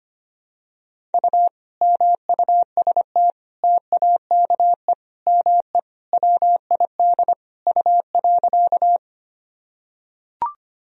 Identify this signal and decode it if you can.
Morse code, heard as a keyed tone, about 25 wpm, 700 Hz. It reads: U MUST TAKE ME WID U.
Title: Sentences from Top 200 Words and Abbreviations 25wpm